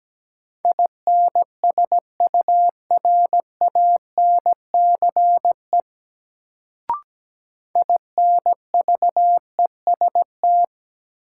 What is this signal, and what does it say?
Morse code, heard as a keyed tone, about 17 wpm, 700 Hz. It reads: INSURANCE INVEST
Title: Sentences from Top 1100 Words - New Word Drill - 17wpm